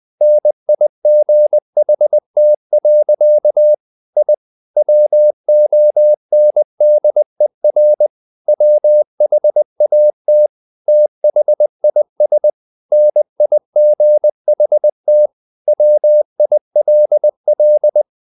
Morse code, heard as a keyed tone, about 20 wpm, 600 Hz. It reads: NIGHT. I WONDER WHAT THIS NIGHT WILL
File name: the old man and the sea part 2 20wpm